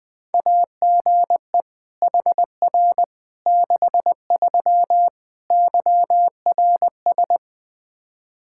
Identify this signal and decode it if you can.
Morse code, heard as a keyed tone, about 21 wpm, 700 Hz. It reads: AGE HR 63 YRS
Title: Practice QSO - Exchange 3 - More Info - 20wpm